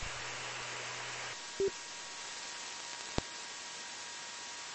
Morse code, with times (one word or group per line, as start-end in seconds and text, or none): none